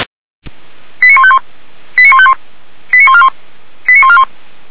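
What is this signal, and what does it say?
Five-tone selective call (ZVEI), heard as quick runs of five tones, tones of about 70 ms. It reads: 89151 89151 89141 89141
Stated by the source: ZVEI five-tone selective call received on SDR.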